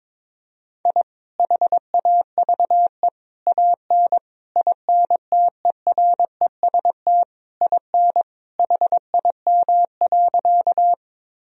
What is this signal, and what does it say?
Morse code, heard as a keyed tone, about 22 wpm, 700 Hz. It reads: I HAVE AN INTEREST IN HIM.
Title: Sentences from Top 1000 Words - Review Part 2 - 22wpm